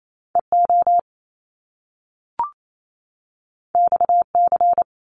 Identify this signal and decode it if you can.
Morse code, heard as a keyed tone, about 28 wpm, 700 Hz. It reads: EO XC
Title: Top Two Letter Combinations in Words 28wpm